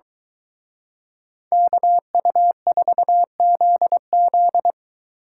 Morse code, heard as a keyed tone, about 23 wpm, 700 Hz. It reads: KU4ZZ